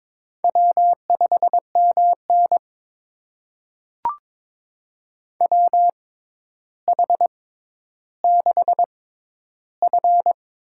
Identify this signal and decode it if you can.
Morse code, heard as a keyed tone, about 22 wpm, spaced as for 11 wpm, 700 Hz. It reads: W5MN WH6F
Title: USA Call Signs - 6x Character Spacing - 22wpm